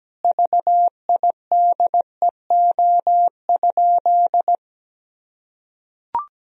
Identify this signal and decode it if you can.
Morse code, heard as a keyed tone, about 17 wpm, 700 Hz. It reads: VIDEO?
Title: Sentences from Top 1000 Words - Review Part 4 - 17wpm